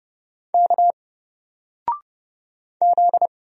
Morse code, heard as a keyed tone, about 30 wpm, 700 Hz. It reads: K Z
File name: Single-Letter-Number-30wpm